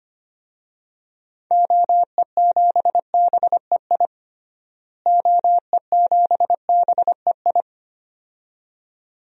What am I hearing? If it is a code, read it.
Morse code, heard as a keyed tone, about 25 wpm, 700 Hz. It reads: OE7BEI OE7BEI
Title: DXCC-100 Call Signs - 6 Characters - Repeated 2x - 25wpm